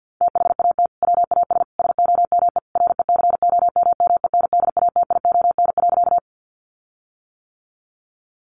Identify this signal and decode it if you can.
Morse code, heard as a keyed tone, about 50 wpm, 700 Hz. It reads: THAT WAS SOME RECOMMENDATION.
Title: Sentences from Top 1400 Words 50wpm